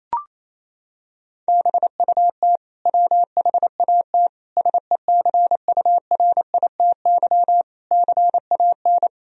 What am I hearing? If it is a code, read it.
Morse code, heard as a keyed tone, about 28 wpm, 700 Hz. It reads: BUT WHAT SECURITY CAN
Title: Sentences from Top 1000 Words - Review Part 4 - 28wpm